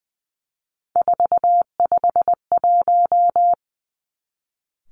Morse code, heard as a keyed tone, about 20 wpm, 700 Hz. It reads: 451
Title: Three Challenging Letter-Number Combinations 20wpm